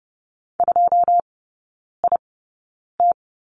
Morse code, heard as a keyed tone, about 30 wpm, spaced as for 9 wpm, 700 Hz. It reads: 2IT
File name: Lesson 18e - Letter m - Three Characters 30wpm